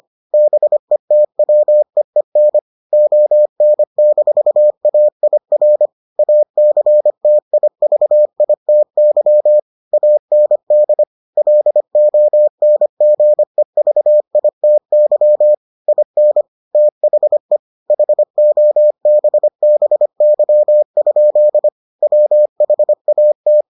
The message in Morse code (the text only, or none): BETWEEN ON-AIR ACTIVITY AND LONGEVITY IN THE HOBBY? WHAT